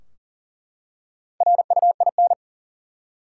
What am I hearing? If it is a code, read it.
Morse code, heard as a keyed tone, about 40 wpm, 700 Hz. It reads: RUIN
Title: Four Letter Words - 40wpm